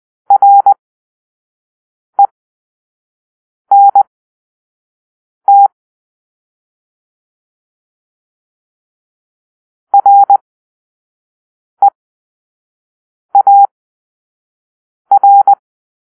Morse code, heard as a keyed tone, about 20 wpm, spaced as for 5 wpm, 800 Hz. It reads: RENT REAR
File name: arrl / Grp-1 Drill-2